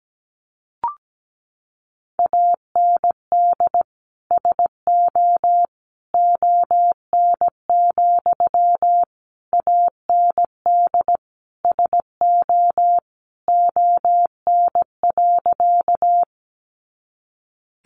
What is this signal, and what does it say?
Morse code, heard as a keyed tone, about 17 wpm, 700 Hz. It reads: AND SO ON, AND SO ON.
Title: Sentences from Top 100 Words 17wpm